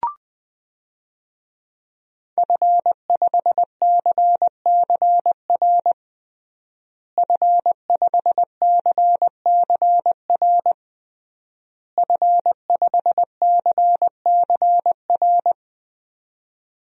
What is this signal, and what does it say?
Morse code, heard as a keyed tone, about 21 wpm, 700 Hz. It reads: F5CCR F5CCR F5CCR
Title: DXCC-100 Call Signs - 5 Characters - Repeated 3x - 20wpm